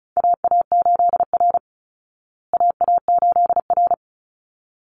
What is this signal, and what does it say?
Morse code, heard as a keyed tone, about 34 wpm, 700 Hz. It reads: AA8R AA8R
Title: Call Signs Repeated 3x - 35wpm